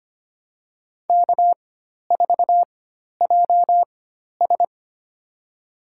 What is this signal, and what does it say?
Morse code, heard as a keyed tone, about 25 wpm, spaced as for 12 wpm, 700 Hz. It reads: K4JS
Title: USA Call Signs - 4 Characters - 4x Character Spacing - 25wpm